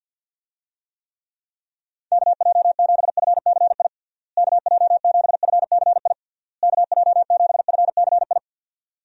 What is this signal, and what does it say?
Morse code, heard as a keyed tone, about 47 wpm, 700 Hz. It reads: KJ7FCI KJ7FCI KJ7FCI
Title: USA Call Signs - 6 Characters - Repeated 3x - 50wpm